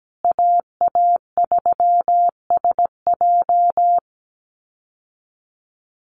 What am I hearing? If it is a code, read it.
Morse code, heard as a keyed tone, about 17 wpm, 700 Hz. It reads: AA3SJ
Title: USA Call Signs Repeated 3x 17wpm